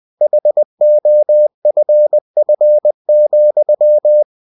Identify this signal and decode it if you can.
Morse code, heard as a keyed tone, about 20 wpm, 600 Hz. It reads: HOFF,